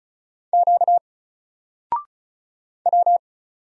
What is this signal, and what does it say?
Morse code, heard as a keyed tone, about 34 wpm, 700 Hz. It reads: Q W
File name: Single-Letter-Number-35wpm